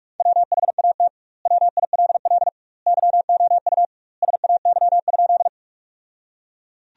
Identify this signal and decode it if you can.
Morse code, heard as a keyed tone, about 46 wpm, 700 Hz. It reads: WHAT WILL YOU SAY?
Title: 2x Word-Spacing - Sentences from Top 200 Words - 45wpm